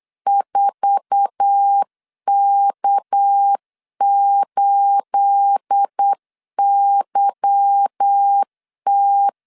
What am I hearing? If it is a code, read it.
Morse code, heard as a keyed tone, about 8 wpm, 800 Hz. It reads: 4K8YT
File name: LEZIONI / 9 28 8wpm 3512